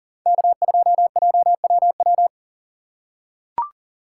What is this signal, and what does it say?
Morse code, heard as a keyed tone, about 40 wpm, 700 Hz. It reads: K2JWW
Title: USA Call Signs - 5 Characters - 3x Character Spacing - 40wpm